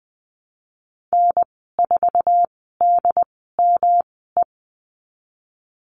Morse code, heard as a keyed tone, about 19 wpm, spaced as for 14 wpm, 700 Hz. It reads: N4DME
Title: 167 Lesson 27g - Number 7 - Call Signs 20wpm